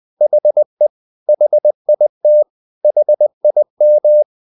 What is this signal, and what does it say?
Morse code, heard as a keyed tone, about 20 wpm, 600 Hz. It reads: HE HIT HIM
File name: the old man and the sea part 2 20wpm